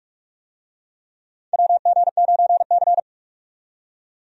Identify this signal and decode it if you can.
Morse code, heard as a keyed tone, about 46 wpm, 700 Hz. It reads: WG9C
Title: USA Call Signs - 4 Characters - 45wpm